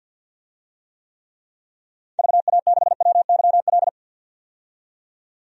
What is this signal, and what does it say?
Morse code, heard as a keyed tone, about 51 wpm, 700 Hz. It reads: VA6WYL
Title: DXCC-100 Call Signs - 6 Characters - 50wpm